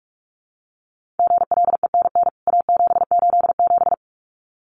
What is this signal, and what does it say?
Morse code, heard as a keyed tone, about 44 wpm, 700 Hz. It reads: GLENN A787
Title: CWT Exchange 45wpm